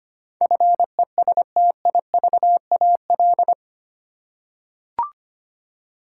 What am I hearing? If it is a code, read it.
Morse code, heard as a keyed tone, about 25 wpm, 700 Hz. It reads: FESTIVAL
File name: Eight Letter Words - 25wpm